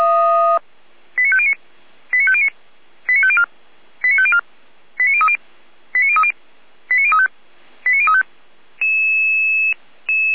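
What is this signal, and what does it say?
Five-tone selective call (ZVEI), heard as quick runs of five tones, tones of about 80 ms. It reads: 89509 89509 89504 89504 89030 89030 89035 89035
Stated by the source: ZVEI five-tone selective call received by SDR, from the Feuerwehr Orlamünde (feuerwehr-orlamuende) fire brigade server.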